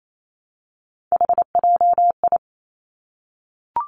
Morse code, heard as a keyed tone, about 28 wpm, 700 Hz. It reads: HJI
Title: Three Challenging Letter-Number Combinations 28wpm